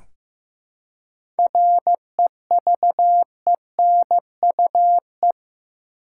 Morse code, heard as a keyed tone, about 15 wpm, 700 Hz. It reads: REVENUE
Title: Seven Letter Words - 15wpm